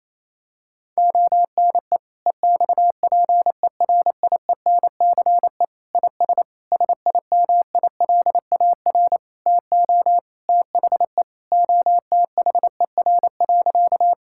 Morse code, heard as a keyed tone, about 28 wpm, 700 Hz. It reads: ONE EXPERIENCE IS SIMILAR TO THE OTHER.
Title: Sentences from Top 1000 Words - Review Part 5 - 28wpm